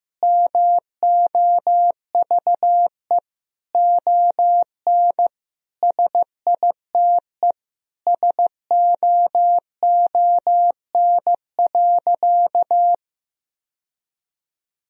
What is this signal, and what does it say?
Morse code, heard as a keyed tone, about 15 wpm, 700 Hz. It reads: MOVE ON SITE SOON.
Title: Sentences from Top 1000 Words - Review Part 4 - 15wpm